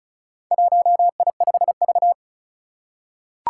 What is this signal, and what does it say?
Morse code, heard as a keyed tone, about 36 wpm, 700 Hz. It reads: 1I5V
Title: Four Challenging Letter-Number Combinations 35wpm